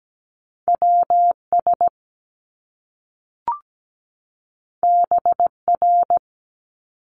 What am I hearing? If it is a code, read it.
Morse code, heard as a keyed tone, about 17 wpm, 700 Hz. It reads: WS BR
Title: Top Two Letter Combinations in Words 17wpm